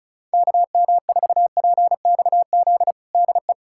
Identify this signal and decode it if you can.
Morse code, heard as a keyed tone, about 36 wpm, 700 Hz. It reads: KM4PXZ DE
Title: USA Callsign DE Callsign - 6 Characters - 35wpm